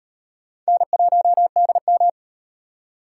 Morse code, heard as a keyed tone, about 36 wpm, 700 Hz. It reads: N1DM